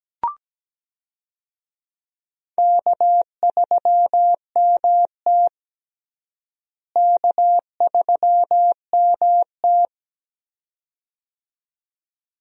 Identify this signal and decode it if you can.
Morse code, heard as a keyed tone, about 17 wpm, 700 Hz. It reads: K3MT K3MT
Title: USA Call Signs - 4 Characters - Repeated 2x - 17wpm